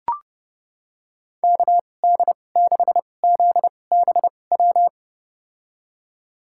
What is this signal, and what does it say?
Morse code, heard as a keyed tone, about 30 wpm, spaced as for 22 wpm, 700 Hz. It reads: KD6ZBW
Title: USA Call Signs - 2x Character Spacing - 30wpm